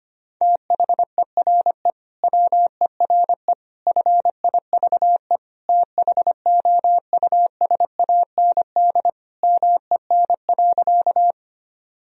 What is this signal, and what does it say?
Morse code, heard as a keyed tone, about 25 wpm, 700 Hz. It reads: THERE WERE FIVE THOUSAND MEN.
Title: Sentences from Top 1000 Words - Review Part 3 - 25wpm